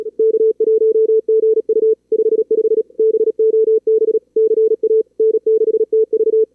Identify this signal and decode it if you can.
Morse code, heard as a keyed tone, about 33 wpm, 440 Hz. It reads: K1GU 55 BOB CA N6TV
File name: CWSprint / Feb2022 / k5tr-feb-na-sprint-cw-k5ot -2300